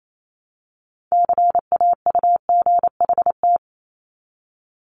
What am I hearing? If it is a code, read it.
Morse code, heard as a keyed tone, about 28 wpm, 700 Hz. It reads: CAUGHT
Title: Multi-letter Phonemes Repeated 3x 28wpm